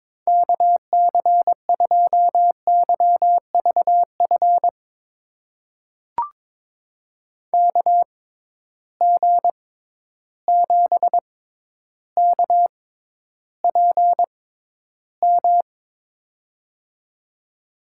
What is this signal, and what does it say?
Morse code, heard as a keyed tone, about 22 wpm, spaced as for 11 wpm, 700 Hz. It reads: KC2YVF KG7KPM
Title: USA Call Signs - 6 Characters - 6x Character Spacing - 22wpm